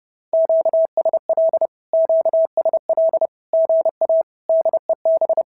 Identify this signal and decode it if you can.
Morse code, heard as a keyed tone, about 30 wpm, 650 Hz. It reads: QSL QSL GA DEB
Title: SOTA QSO Practice 30wpm